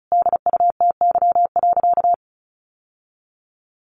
Morse code, heard as a keyed tone, about 34 wpm, 700 Hz. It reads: DUTY.
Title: Sentences from Top 1500 Words 35wpm